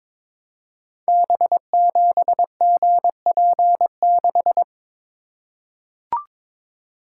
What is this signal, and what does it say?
Morse code, heard as a keyed tone, about 22 wpm, 700 Hz. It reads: B7GP6
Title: Code Groups - 5 Letter:Numbers - 22wpm